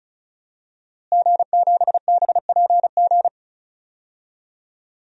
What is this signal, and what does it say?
Morse code, heard as a keyed tone, about 36 wpm, 700 Hz. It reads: G7BPG
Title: DXCC-100 Call Signs - 5 Characters - Repeated 2x - 35wpm